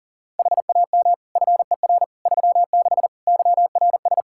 Morse code, heard as a keyed tone, about 40 wpm, 700 Hz. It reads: HAM FER 36 YRS
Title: Practice QSO - Exchange 3 - More Info - 40wpm